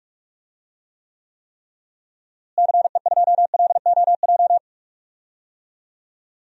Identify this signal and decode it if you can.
Morse code, heard as a keyed tone, about 46 wpm, 700 Hz. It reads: CE2LOJ
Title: DXCC-100 Call Signs - 6 Characters - Repeated 3x - 45wpm